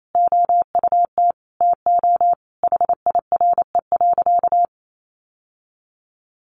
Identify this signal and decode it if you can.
Morse code, heard as a keyed tone, about 28 wpm, 700 Hz. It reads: OUT TO HIRE.
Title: Sentences from Top 1400 Words 28wpm